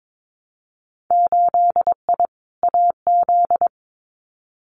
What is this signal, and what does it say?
Morse code, heard as a keyed tone, about 22 wpm, 700 Hz. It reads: 8I AZ
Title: Winter Field Day Exchanges Encore 22wpm